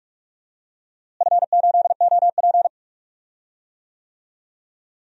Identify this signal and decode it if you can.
Morse code, heard as a keyed tone, about 41 wpm, 700 Hz. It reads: F8OP
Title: DXCC-100 Call Signs - 4 Characters - Repeated 3x - 45wpm